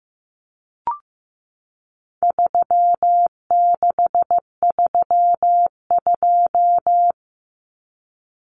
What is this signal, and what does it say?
Morse code, heard as a keyed tone, about 15 wpm, 700 Hz. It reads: 3632
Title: Common Cut Numbers - 4-Digit 15wpm